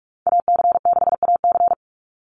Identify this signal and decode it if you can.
Morse code, heard as a keyed tone, about 44 wpm, 700 Hz. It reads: AC6AC
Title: Call Signs - 45wpm